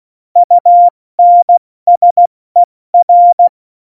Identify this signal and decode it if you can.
Morse code, heard as a keyed tone, about 16 wpm, spaced as for 14 wpm, 700 Hz. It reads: UNSER
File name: Morsekurs / Witze / Witz 09. Oktober 2018, 70 F80